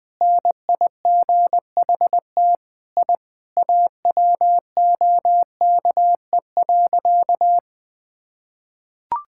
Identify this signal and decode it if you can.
Morse code, heard as a keyed tone, about 20 wpm, 700 Hz. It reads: NIGHT I AWOKE.
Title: Sentences with 4-Word Phrases - 20wpm